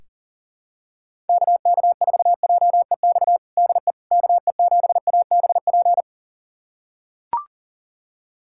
Morse code, heard as a keyed tone, about 40 wpm, 700 Hz. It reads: KK4JEX DE KE7ABP
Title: USA Callsign DE Callsign - 6 Characters - 40wpm